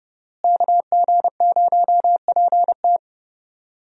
Morse code, heard as a keyed tone, about 31 wpm, 700 Hz. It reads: KG0PT
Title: USA Call Signs - 5 Characters - Repeated 3x - 30wpm